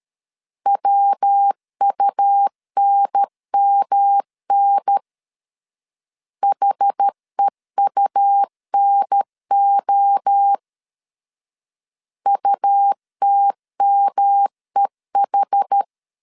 Morse code, heard as a keyed tone, about 13 wpm, 800 Hz. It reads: WUNMN HEUNO UTMEH
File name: CW / DET / 1 3 1500C S C 12wpm 2750